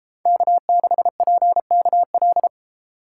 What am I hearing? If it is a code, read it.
Morse code, heard as a keyed tone, about 30 wpm, 700 Hz. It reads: K6PKL